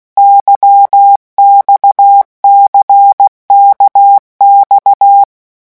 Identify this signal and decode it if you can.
Morse code, heard as a keyed tone, about 16 wpm, 800 Hz. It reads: YXCKX
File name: morse mp3 / revision1-vitesse 15 mots